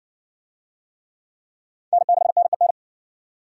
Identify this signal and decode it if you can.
Morse code, heard as a keyed tone, about 53 wpm, 700 Hz. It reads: N6NEN